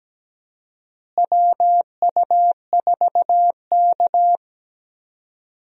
Morse code, heard as a keyed tone, about 17 wpm, 700 Hz. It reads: WU4K
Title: USA Call Signs - 4 Characters - Repeated 3x - 17wpm